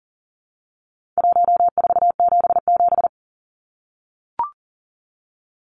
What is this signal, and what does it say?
Morse code, heard as a keyed tone, about 40 wpm, 700 Hz. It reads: 1477
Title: Numbers - Contest Style - 4-Digit 40wpm